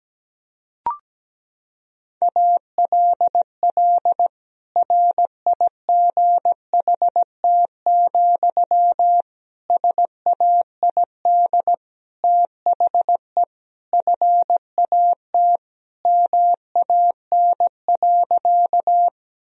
Morse code, heard as a keyed tone, about 17 wpm, 700 Hz. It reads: ALL RIGHT, SAID THE FAT MAN.
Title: Sentences from Top 1000 Words - Review Part 4 - 17wpm